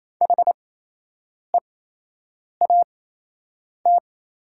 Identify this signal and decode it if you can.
Morse code, heard as a keyed tone, about 28 wpm, spaced as for 8 wpm, 690 Hz. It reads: HEAT
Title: 8x Character Spacing - Top 500 Words - 28wpm